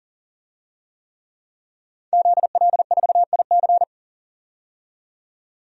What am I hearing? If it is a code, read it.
Morse code, heard as a keyed tone, about 40 wpm, 700 Hz. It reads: ZL4IC
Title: DXCC-100 Call Signs - 5 Characters - 40wpm